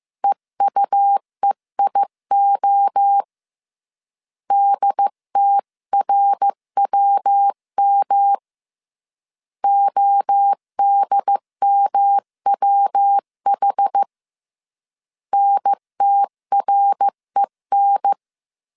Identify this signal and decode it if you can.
Morse code, heard as a keyed tone, about 15 wpm, 800 Hz. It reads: EUEIO DTRWM ODMWH NTREN
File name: CW / DET / 1 3 1500C S C 14wpm 2458